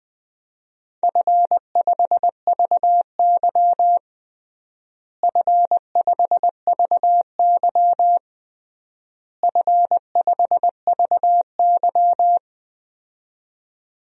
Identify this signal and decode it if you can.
Morse code, heard as a keyed tone, about 20 wpm, 700 Hz. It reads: F5VY F5VY F5VY
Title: DXCC-100 Call Signs - 4 Characters - Repeated 3x - 20wpm